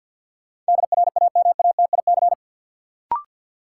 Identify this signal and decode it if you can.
Morse code, heard as a keyed tone, about 53 wpm, 700 Hz. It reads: DRAMATIC
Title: Sentences from Top 1300 Words - New Word Drill Alt - 50wpm